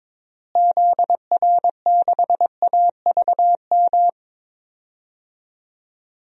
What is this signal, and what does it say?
Morse code, heard as a keyed tone, about 22 wpm, 700 Hz. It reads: ZR6AVM
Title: DXCC-100 Call Signs - 6 Characters - Repeated 3x - 22wpm